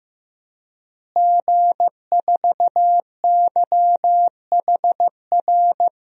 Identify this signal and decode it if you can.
Morse code, heard as a keyed tone, about 15 wpm, 700 Hz. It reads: G4YHR
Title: DXCC-100 Call Signs - 5 Characters - Repeated 3x - 15wpm